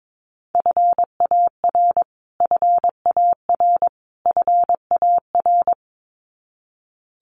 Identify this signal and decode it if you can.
Morse code, heard as a keyed tone, about 22 wpm, 700 Hz. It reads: FAR FAR FAR
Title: Top 500 Words Repeated 22wpm